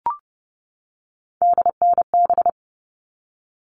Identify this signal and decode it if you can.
Morse code, heard as a keyed tone, about 30 wpm, 700 Hz. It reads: DNB
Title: All Cut Numbers - 3-Digit 30wpm